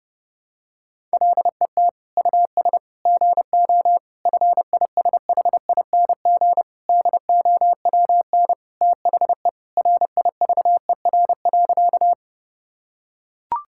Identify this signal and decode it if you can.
Morse code, heard as a keyed tone, about 30 wpm, 700 Hz. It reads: LET US GO FISHING DOWN THE RIVER.
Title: Sentences from Top 1000 Words - Review Part 5 - 30wpm